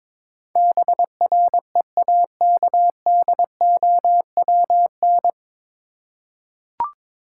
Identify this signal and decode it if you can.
Morse code, heard as a keyed tone, about 22 wpm, 700 Hz. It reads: BREAKDOWN
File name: 9 Letter Words - First 3 Letters Given - 22wpm